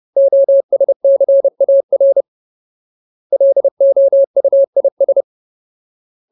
Morse code, heard as a keyed tone, about 30 wpm, 550 Hz. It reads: OSCAR LOUIS
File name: Ultimate Word List / 30WPM Ultimate